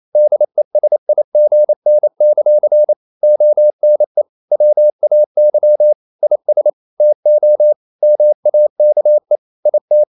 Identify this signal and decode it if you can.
Morse code, heard as a keyed tone, about 28 wpm, 600 Hz. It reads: DESIGN; ONE WAY IS TO MAKE IT
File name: CW / Practice / EnglishQuotations0228